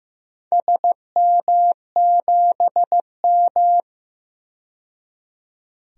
Morse code, heard as a keyed tone, about 15 wpm, 700 Hz. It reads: SM7M
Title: DXCC-100 Call Signs - 4 Characters - 15wpm